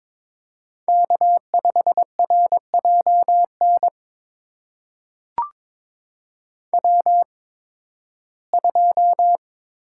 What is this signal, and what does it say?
Morse code, heard as a keyed tone, about 22 wpm, spaced as for 14 wpm, 700 Hz. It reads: K5RJN W2
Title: USA Call Signs - 5 Characters - 8x Character Spacing - 22wpm